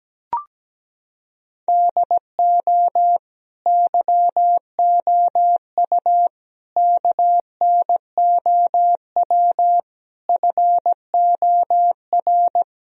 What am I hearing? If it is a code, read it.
Morse code, heard as a keyed tone, about 17 wpm, 700 Hz. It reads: DO YOU KNOW FOR
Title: Sentences from Top 1000 Words - Review Part 2 - 17wpm